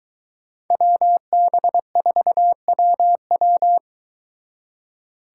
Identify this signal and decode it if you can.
Morse code, heard as a keyed tone, about 23 wpm, 700 Hz. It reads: WB4WW